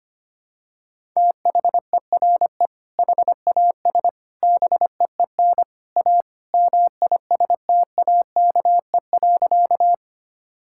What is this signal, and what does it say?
Morse code, heard as a keyed tone, about 25 wpm, 700 Hz. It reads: THERE HAS BEEN A MISTAKE.
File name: practice / 25wpm / Sentences with 3-Word Phrases - 25wpm